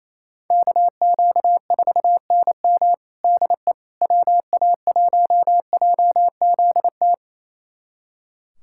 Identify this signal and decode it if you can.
Morse code, heard as a keyed tone, about 28 wpm, 700 Hz. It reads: KQ4NM DE WA1JZT
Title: USA Callsign DE Callsign 28wpm